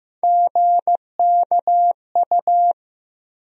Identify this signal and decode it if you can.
Morse code, heard as a keyed tone, about 15 wpm, 700 Hz. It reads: GKU
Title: USA Call Sign Postfixes - 3 Letters 15wpm